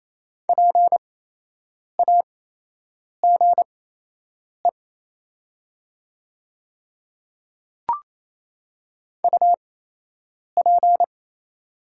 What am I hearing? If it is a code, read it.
Morse code, heard as a keyed tone, about 28 wpm, spaced as for 8 wpm, 690 Hz. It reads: PAGE UP
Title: 8x Character Spacing - Top 300 Words - 28wpm